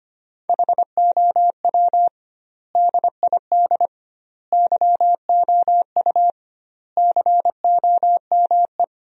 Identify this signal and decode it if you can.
Morse code, heard as a keyed tone, about 25 wpm, 700 Hz. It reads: HOW DID YOU COME
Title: 2x Word-Spacing - Sentences from Top 100 Words - 25wpm